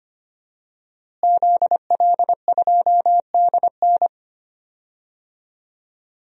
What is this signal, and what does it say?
Morse code, heard as a keyed tone, about 25 wpm, 700 Hz. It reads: ZL2DN
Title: DXCC-100 Call Signs - 5 Characters - Repeated 2x - 25wpm